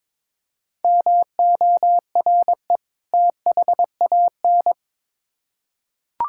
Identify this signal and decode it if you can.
Morse code, heard as a keyed tone, about 22 wpm, 700 Hz. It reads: MORE THAN
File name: Build A Phrase - 2-Word Phrases - 22wpm